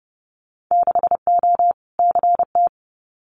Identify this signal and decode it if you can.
Morse code, heard as a keyed tone, about 30 wpm, 700 Hz. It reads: 6O CT
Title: Winter Field Day Exchanges Encore 30wpm